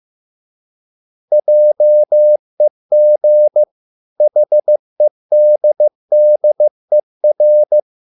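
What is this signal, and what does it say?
Morse code, heard as a keyed tone, about 15 wpm, 600 Hz. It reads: JEG HEDDER
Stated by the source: Morse code at a slow 15 wpm from CW med Gnister.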